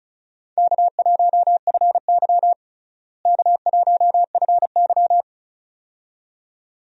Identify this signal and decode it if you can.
Morse code, heard as a keyed tone, about 34 wpm, 700 Hz. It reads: K1FY K1FY
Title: USA Call Signs - 4 Characters - Repeated 2x - 35wpm